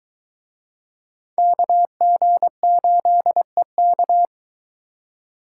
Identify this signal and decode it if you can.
Morse code, heard as a keyed tone, about 23 wpm, 700 Hz. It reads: KG8EK